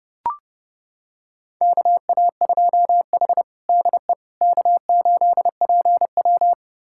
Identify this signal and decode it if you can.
Morse code, heard as a keyed tone, about 30 wpm, 700 Hz. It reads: KA2H DE K8PW
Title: USA Callsign DE Callsign - 4 Characters - 30wpm